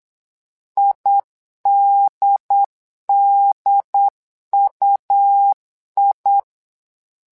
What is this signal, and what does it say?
Morse code, heard as a keyed tone, about 8 wpm, 800 Hz. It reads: IDDUI
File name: LEZIONI / 2 UHDN / 2 UHD 8wpm 2258